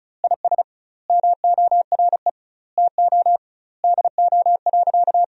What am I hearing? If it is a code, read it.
Morse code, heard as a keyed tone, about 36 wpm, 700 Hz. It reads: IS MORE TO DO.
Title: 2x Word-Spacing - Sentences from Top 100 Words - 35wpm